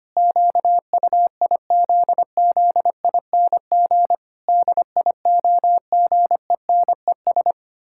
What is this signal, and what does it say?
Morse code, heard as a keyed tone, about 25 wpm, 700 Hz. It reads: QUIZZING DIOGENES
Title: Pangram Challenge v2 - 25wpm